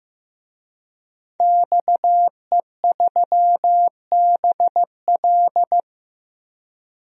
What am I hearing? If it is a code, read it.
Morse code, heard as a keyed tone, about 15 wpm, 700 Hz. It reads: XE3BL
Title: DXCC-100 Call Signs - 5 Characters - 15wpm